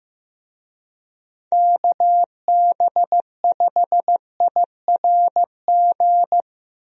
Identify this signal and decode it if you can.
Morse code, heard as a keyed tone, about 15 wpm, 700 Hz. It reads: KB5IRG
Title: USA Call Signs - 6 Characters - Repeated 3x - 15wpm